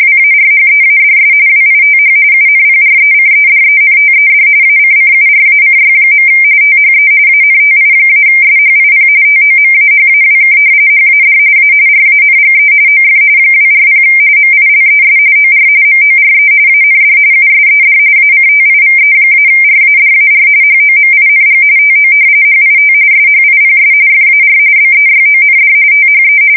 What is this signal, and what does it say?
RTTY, about 45.45 baud. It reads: WARHEADS WITH UNPRECEDENTED ACCURACY. THE ANTI-SHIP TOMAHAWK HAD A RANGE OF 240 MILES OR MORE AND USED ACTIVE-RADAR HOMING TO HIT SHIPS FAR BEYOND THE
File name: fsk170 / battleshipnj